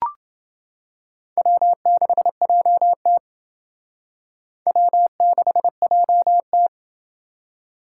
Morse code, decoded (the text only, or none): W6JT W6JT